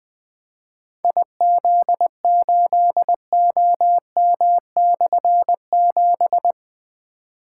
Morse code, heard as a keyed tone, about 20 wpm, 700 Hz. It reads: IZ8OM/7
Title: Difficult Call Signs 20wpm